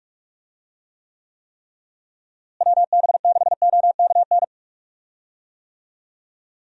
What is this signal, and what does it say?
Morse code, heard as a keyed tone, about 43 wpm, 700 Hz. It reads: WB6OKN
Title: USA Call Signs - 6 Characters - Repeated 3x - 45wpm